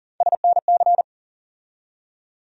Morse code, heard as a keyed tone, about 40 wpm, 700 Hz. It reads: SNC
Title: USA Call Sign Postfixes - 3 Letters 40wpm